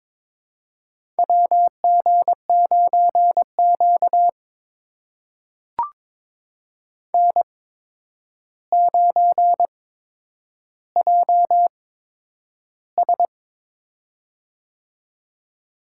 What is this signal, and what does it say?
Morse code, heard as a keyed tone, about 23 wpm, spaced as for 9 wpm, 700 Hz. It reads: WG9Q N9JS
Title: USA Call Signs - 4 Characters - 8x Character Spacing - 22wpm